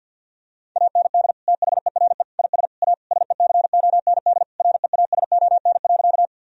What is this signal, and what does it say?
Morse code, heard as a keyed tone, about 50 wpm, 700 Hz. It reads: AND THERE IS A SECOND REASON.
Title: Sentences from Top 1000 Words - Review Part 1 - 50wpm